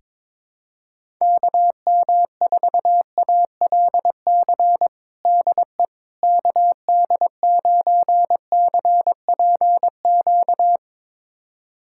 Morse code, heard as a keyed tone, about 22 wpm, 700 Hz. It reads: KM4ALC DE KD9CPQ
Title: USA Callsign DE Callsign 22wpm